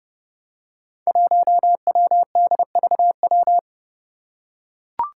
Morse code, heard as a keyed tone, about 30 wpm, 700 Hz. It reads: 1WDVW
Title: Five Challenging Letter-Number Combinations 30wpm